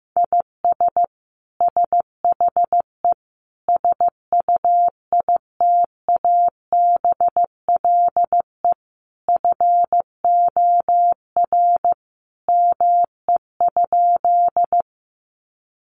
Morse code, heard as a keyed tone, about 15 wpm, 700 Hz. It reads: IS SHE SUITABLE FOR ME?
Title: Sentences from Top 1400 Words 15wpm